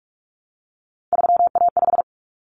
Morse code, heard as a keyed tone, about 44 wpm, 700 Hz. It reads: 3A5
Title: Common Cut Numbers - 3-Digit 45wpm